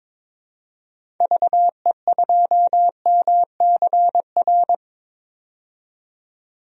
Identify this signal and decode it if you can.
Morse code, heard as a keyed tone, about 22 wpm, 700 Hz. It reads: VE2MCR
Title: DXCC-100 Call Signs - 6 Characters - 22wpm